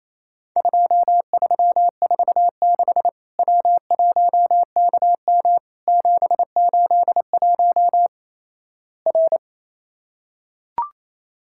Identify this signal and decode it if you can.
Morse code, heard as a keyed tone, about 28 wpm, 700 Hz. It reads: 2346 W1KM 781 R
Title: Worked All Europe DX Contest - QTC Exchange - 28wpm